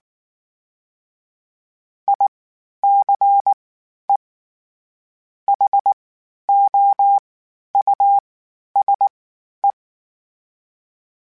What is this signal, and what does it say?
Morse code, heard as a keyed tone, about 19 wpm, 800 Hz. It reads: I C E H O U S E
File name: General / General-mp3s / Houses @18